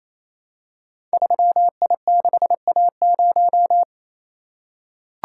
Morse code, heard as a keyed tone, about 27 wpm, 700 Hz. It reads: 3I6A0
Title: Code Groups - 5 Letter:Numbers - 28wpm